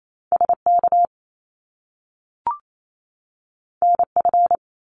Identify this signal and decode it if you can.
Morse code, heard as a keyed tone, about 28 wpm, 700 Hz. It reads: SK NF
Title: Top Two Letter Combinations in Words 28wpm